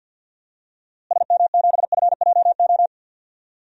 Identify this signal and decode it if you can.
Morse code, heard as a keyed tone, about 52 wpm, 700 Hz. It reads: SM7FJO
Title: DXCC-100 Call Signs - 6 Characters - Repeated 2x - 50wpm